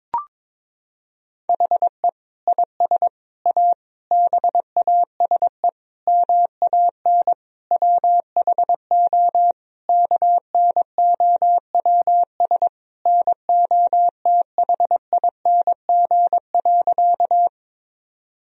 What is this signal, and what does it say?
Morse code, heard as a keyed tone, about 22 wpm, 700 Hz. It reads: HE IS A BASE MAN WHO KNOWS NOTHING.
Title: Sentences from Top 1000 Words - Review Part 1 - 22wpm